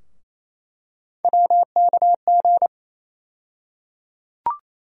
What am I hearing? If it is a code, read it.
Morse code, heard as a keyed tone, about 27 wpm, 700 Hz. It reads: WKG
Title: USA Call Sign Postfixes - 3 Letters 28wpm